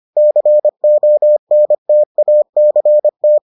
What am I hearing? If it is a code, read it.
Morse code, heard as a keyed tone, about 25 wpm, 600 Hz. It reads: CONTACT